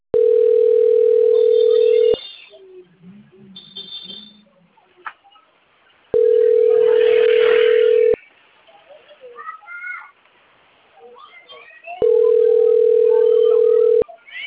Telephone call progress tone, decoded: ringback tone

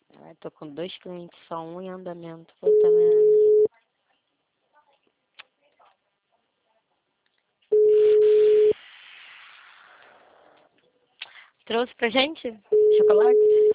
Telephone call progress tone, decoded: ringback tone

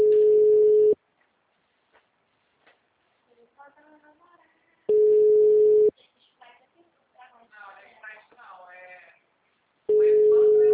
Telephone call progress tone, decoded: ringback tone

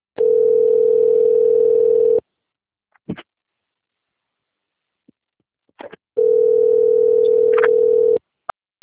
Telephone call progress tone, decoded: ringback tone

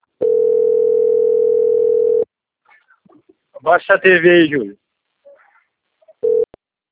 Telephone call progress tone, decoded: ringback tone

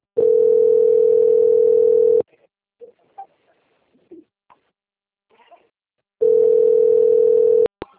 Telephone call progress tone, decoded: ringback tone

